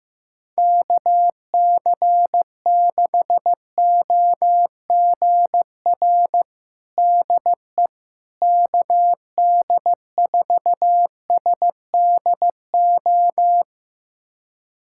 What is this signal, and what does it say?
Morse code, heard as a keyed tone, about 16 wpm, 700 Hz. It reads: KC6OGR DE KD4SDO
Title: USA Callsign DE Callsign 15wpm